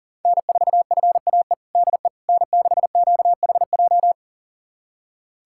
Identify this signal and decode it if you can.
Morse code, heard as a keyed tone, about 40 wpm, 700 Hz. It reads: N4FAE DE N6QHJ
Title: USA Callsign DE Callsign - 5 Characters - 40wpm